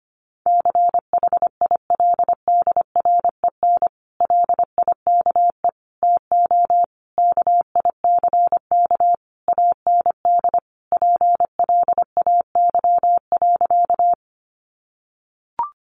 Morse code, heard as a keyed tone, about 25 wpm, 700 Hz. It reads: CHILDREN LIKE TO KICK AND PLAY.
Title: Sentences from Top 1300 Words 25wpm